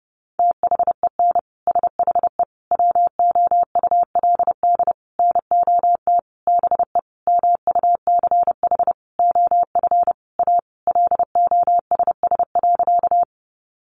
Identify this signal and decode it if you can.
Morse code, heard as a keyed tone, about 30 wpm, 700 Hz. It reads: THEN SHE WOULD NOT BE MUCH OF A LOSS.
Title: Sentences from Top 1400 Words 30wpm